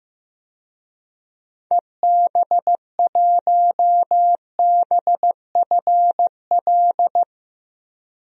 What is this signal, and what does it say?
Morse code, heard as a keyed tone, about 15 wpm, 700 Hz. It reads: EB1BFL
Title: DXCC-100 Call Signs - 6 Characters - Repeated 2x - 15wpm